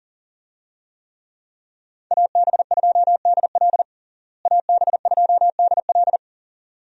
Morse code, heard as a keyed tone, about 37 wpm, 700 Hz. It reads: AB2DL AB2DL
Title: USA Call Signs - 5 Characters - Repeated 2x - 40wpm